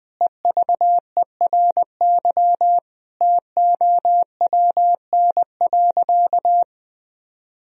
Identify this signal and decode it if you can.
Morse code, heard as a keyed tone, about 21 wpm, 700 Hz. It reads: EVERY TOWN.
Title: Sentences from Top 1000 Words - Review Part 4 - 20wpm